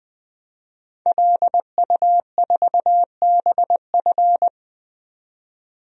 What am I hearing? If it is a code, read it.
Morse code, heard as a keyed tone, about 21 wpm, 700 Hz. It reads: LU4BF
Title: DXCC-100 Call Signs - 5 Characters - 20wpm